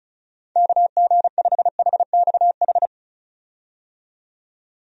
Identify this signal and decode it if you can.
Morse code, heard as a keyed tone, about 36 wpm, 700 Hz. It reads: KG5HXH
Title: USA Call Signs - 6 Characters - 35wpm